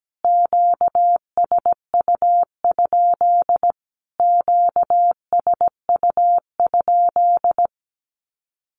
Morse code, heard as a keyed tone, about 17 wpm, 700 Hz. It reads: QSU? QSU?
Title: Q-Codes Repeated 3x 17wpm